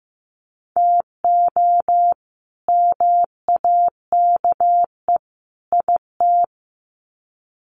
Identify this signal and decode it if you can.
Morse code, heard as a keyed tone, about 15 wpm, 700 Hz. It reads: TO MAKE IT
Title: Sets of 3 Words 15wpm